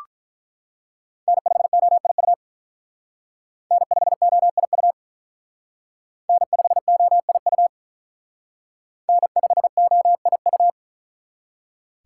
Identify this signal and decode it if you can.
Morse code, heard as a keyed tone, about 44 wpm, 700 Hz. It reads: N5OIU N5OIU N5OIU N5OIU